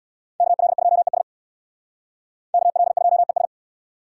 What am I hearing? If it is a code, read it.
Morse code, heard as a keyed tone, about 62 wpm, 700 Hz. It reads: KL2ES KL2ES
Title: Speed Racing - USA 2x2 Call Signs 50wpm